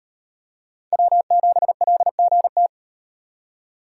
Morse code, heard as a keyed tone, about 37 wpm, 700 Hz. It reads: W7LGT